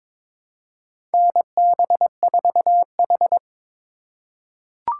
Morse code, heard as a keyed tone, about 22 wpm, 700 Hz. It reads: NB4H
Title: USA Call Signs - 4 Characters - 8x Character Spacing - 22wpm